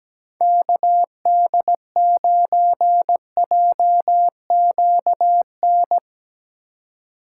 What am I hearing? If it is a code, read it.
Morse code, heard as a keyed tone, about 17 wpm, 700 Hz. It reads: KD9JQN
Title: USA Call Signs - 6 Characters - 4x Character Spacing - 17wpm